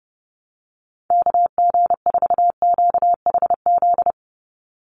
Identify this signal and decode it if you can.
Morse code, heard as a keyed tone, about 30 wpm, 700 Hz. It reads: KG4QHZ
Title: USA Call Signs 30wpm